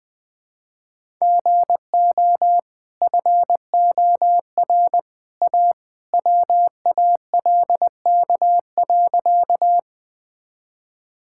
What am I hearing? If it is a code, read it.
Morse code, heard as a keyed tone, about 20 wpm, 700 Hz. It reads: GO FOR A WALK.
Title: Easy Sentences with Hints - 20wpm